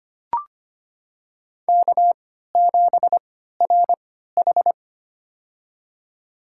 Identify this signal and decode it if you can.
Morse code, heard as a keyed tone, about 25 wpm, spaced as for 14 wpm, 700 Hz. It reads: K7RH
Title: USA Call Signs - 4 Characters - 3x Character Spacing - 25wpm